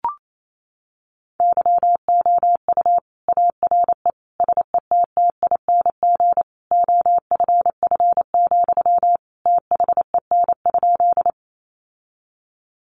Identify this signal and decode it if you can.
Morse code, heard as a keyed tone, about 28 wpm, 700 Hz. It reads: YOU ARE SETTING OFF, THEN?
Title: Sentences from Top 1300 Words 28wpm